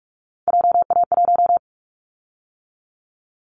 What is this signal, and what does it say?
Morse code, heard as a keyed tone, about 44 wpm, 700 Hz. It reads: JA1
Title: Callsign Prefixes for Top 50 DXCC - 45wpm